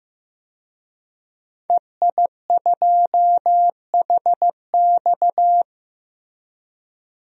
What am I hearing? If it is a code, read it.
Morse code, heard as a keyed tone, about 15 wpm, 700 Hz. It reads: EI2HX
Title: DXCC-100 Call Signs - 5 Characters - Repeated 2x - 15wpm